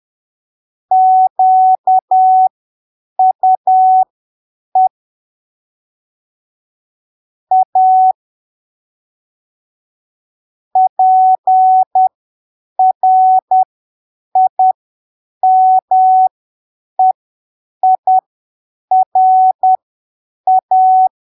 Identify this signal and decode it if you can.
Morse code, heard as a keyed tone, about 10 wpm, spaced as for 7 wpm, 750 Hz. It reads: QUE A PRIMEIRA